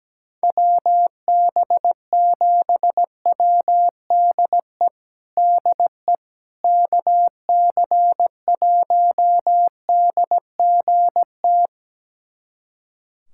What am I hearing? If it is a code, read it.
Morse code, heard as a keyed tone, about 17 wpm, 700 Hz. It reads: WB7WDE DE KC1DGT
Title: USA Callsign DE Callsign - 6 Characters - 17wpm